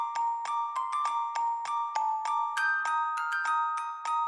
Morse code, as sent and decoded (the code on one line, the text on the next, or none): none